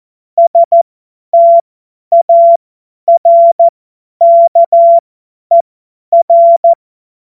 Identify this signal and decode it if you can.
Morse code, heard as a keyed tone, about 14 wpm, spaced as for 10 wpm, 680 Hz. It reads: STARKER
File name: Morsekurs / Witze / Witz 25. April 2023 50 F70